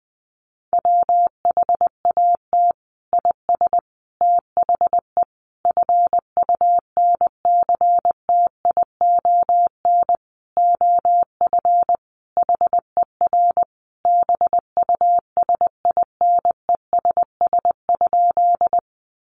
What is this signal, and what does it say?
Morse code, heard as a keyed tone, about 19 wpm, 700 Hz. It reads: WHAT IS THE FUNCTION OF HER BUSINESS?
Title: Sentences from Top 1100 Words 20wpm